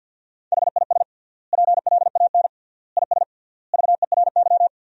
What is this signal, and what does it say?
Morse code, heard as a keyed tone, about 52 wpm, 700 Hz. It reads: HIS PLAN IS VERY
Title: 3x Word-Spacing - Sentences from Top 500 Words - 50wpm